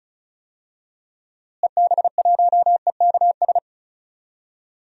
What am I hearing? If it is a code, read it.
Morse code, heard as a keyed tone, about 36 wpm, 700 Hz. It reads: EB1EKS
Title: DXCC-100 Call Signs - 6 Characters - Repeated 2x - 35wpm